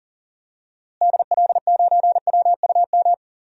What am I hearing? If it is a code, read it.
Morse code, heard as a keyed tone, about 39 wpm, 700 Hz. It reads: DL9WUM
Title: DXCC-100 Call Signs - 6 Characters - Repeated 2x - 40wpm